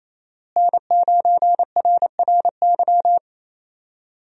Morse code, heard as a keyed tone, about 28 wpm, 700 Hz. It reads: N9RRY
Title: USA Call Signs - 5 Characters - Repeated 2x - 28wpm